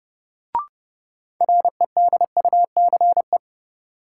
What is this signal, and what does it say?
Morse code, heard as a keyed tone, about 32 wpm, 700 Hz. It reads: REDUCE
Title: Sentences from Top 1100 Words - New Word Drill Alt - 30wpm